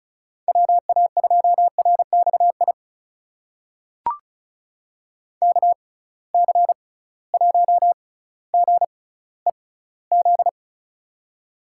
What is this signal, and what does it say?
Morse code, heard as a keyed tone, about 36 wpm, spaced as for 18 wpm, 700 Hz. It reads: WA2RXI KC1GEZ
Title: USA Call Signs - 6 Characters - 6x Character Spacing - 35wpm